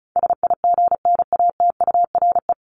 Morse code, heard as a keyed tone, about 34 wpm, 700 Hz. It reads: SIGNATURE
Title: Top 1751-2000 Words 35wpm